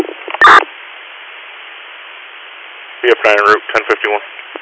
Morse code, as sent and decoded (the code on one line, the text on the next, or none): none